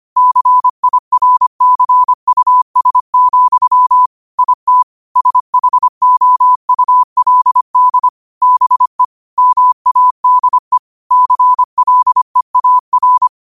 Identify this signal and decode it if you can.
Morse code, heard as a keyed tone, about 25 wpm, 1000 Hz. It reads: CIRCUS, IT SHOULD BE MADE CLEAR